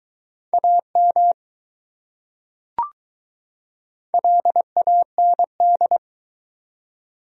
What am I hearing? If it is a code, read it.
Morse code, heard as a keyed tone, about 20 wpm, 700 Hz. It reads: AM LAND